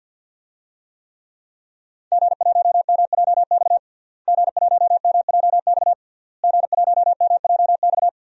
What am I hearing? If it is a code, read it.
Morse code, heard as a keyed tone, about 51 wpm, 700 Hz. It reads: G1MJX G1MJX G1MJX
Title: DXCC-100 Call Signs - 5 Characters - Repeated 3x - 50wpm